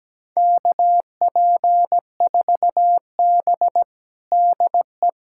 Morse code, heard as a keyed tone, about 17 wpm, 700 Hz. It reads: KP4B DE
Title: USA Callsign DE Callsign - 4 Characters - 17wpm